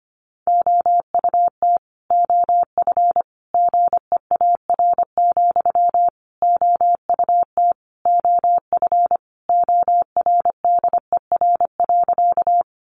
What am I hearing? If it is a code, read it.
Morse code, heard as a keyed tone, about 25 wpm, 700 Hz. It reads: OUT OF GEAR, OUT OF ORDER.
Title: Sentences from Top 1400 Words 25wpm